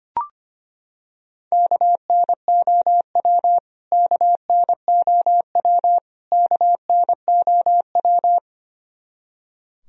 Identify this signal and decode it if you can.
Morse code, heard as a keyed tone, about 25 wpm, 700 Hz. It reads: KNOW KNOW KNOW
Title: Top 100 Words Repeated 25wpm